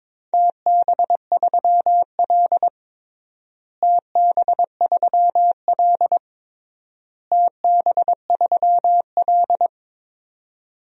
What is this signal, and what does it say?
Morse code, heard as a keyed tone, about 22 wpm, 700 Hz. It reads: TB3L TB3L TB3L
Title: DXCC-100 Call Signs - 4 Characters - Repeated 3x - 22wpm